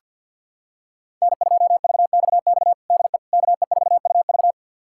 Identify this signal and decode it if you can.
Morse code, heard as a keyed tone, about 47 wpm, 700 Hz. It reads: N2VXX DE KE4UV
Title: USA Callsign DE Callsign - 5 Characters - 50wpm